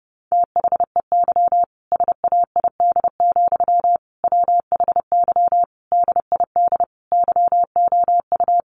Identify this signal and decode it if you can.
Morse code, heard as a keyed tone, about 30 wpm, 700 Hz. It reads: THEY SAID, WHY DID YOU
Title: Sentences from Top 1200 Words 30wpm